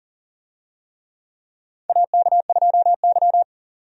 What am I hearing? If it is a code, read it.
Morse code, heard as a keyed tone, about 40 wpm, 700 Hz. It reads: AK2Y
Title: USA Call Signs - 4 Characters - 40wpm